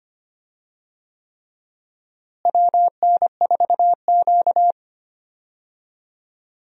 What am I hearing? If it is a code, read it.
Morse code, heard as a keyed tone, about 25 wpm, 700 Hz. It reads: WN4Q
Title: USA Call Signs - 4 Characters - Repeated 3x - 25wpm